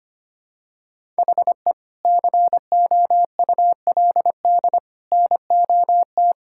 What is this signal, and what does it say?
Morse code, heard as a keyed tone, about 25 wpm, 700 Hz. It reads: HE COULD NOT
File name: practice / 25wpm / Sentences with 3-Word Phrases - 25wpm